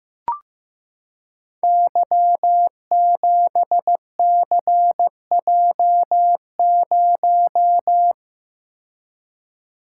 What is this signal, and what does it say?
Morse code, heard as a keyed tone, about 15 wpm, 700 Hz. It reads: Y7CJ0
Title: Code Groups - 5 Letter:Numbers - 15wpm